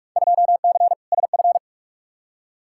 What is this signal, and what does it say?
Morse code, heard as a keyed tone, about 47 wpm, 700 Hz. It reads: 2C SF
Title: Field Day Exchanges Encore - 45wpm